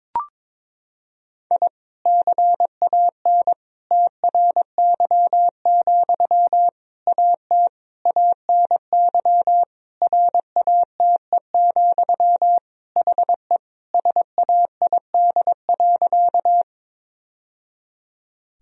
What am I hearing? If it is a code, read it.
Morse code, heard as a keyed tone, about 22 wpm, 700 Hz. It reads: I CAN TRY, AT ANY RATE, HE SAID.
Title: Sentences from Top 1000 Words - Review Part 4 - 22wpm